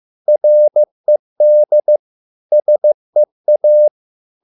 Morse code, heard as a keyed tone, about 15 wpm, 600 Hz. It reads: RED SEA